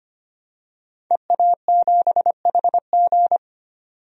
Morse code, heard as a keyed tone, about 25 wpm, 700 Hz. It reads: EA7HG
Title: DXCC-100 Call Signs - 5 Characters - 25wpm